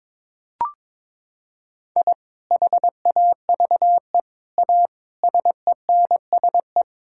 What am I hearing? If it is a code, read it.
Morse code, heard as a keyed tone, about 22 wpm, 700 Hz. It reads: I HAVE A SENSE
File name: practice / 22wpm / Sentences with 3-Word Phrases - 22wpm